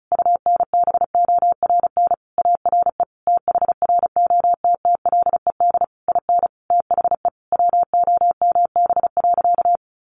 Morse code, heard as a keyed tone, about 34 wpm, 700 Hz. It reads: UNBORN ARE THROTTLED IN THE WOMB.
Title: A Sound Of Thunder 35wpm